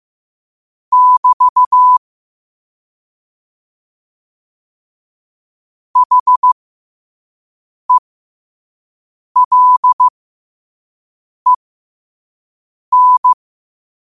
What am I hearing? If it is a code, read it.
Morse code, heard as a keyed tone, about 15 wpm, spaced as for 5 wpm, 1000 Hz. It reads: <BT> HELEN
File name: qotdfiles / 5wpm